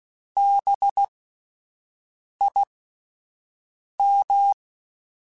Morse code, heard as a keyed tone, about 16 wpm, spaced as for 5 wpm, 800 Hz. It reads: BIM
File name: Spaced Alphas / Spaced Alphas mp3s / Bs sp @10